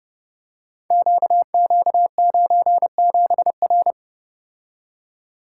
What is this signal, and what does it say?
Morse code, heard as a keyed tone, about 30 wpm, 700 Hz. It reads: QQ97R
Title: Code Groups - 5 Letter:Numbers - 30wpm